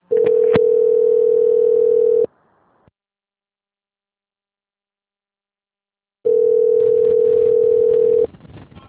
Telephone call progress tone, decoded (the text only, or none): ringback tone